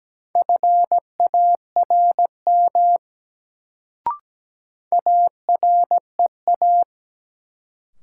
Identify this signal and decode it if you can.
Morse code, heard as a keyed tone, about 17 wpm, 700 Hz. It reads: FARM AREA
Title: Sentences from Top 1000 Words - Review Part 2 - New Word Drill - 17wpm